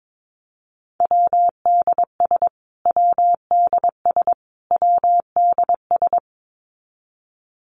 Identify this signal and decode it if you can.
Morse code, heard as a keyed tone, about 22 wpm, 700 Hz. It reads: WDS WDS WDS
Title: Abbreviations Repeated 3x - 22wpm